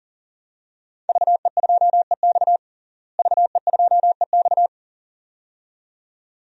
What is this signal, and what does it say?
Morse code, heard as a keyed tone, about 40 wpm, 700 Hz. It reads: VE2EX VE2EX
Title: DXCC-100 Call Signs - 5 Characters - Repeated 2x - 40wpm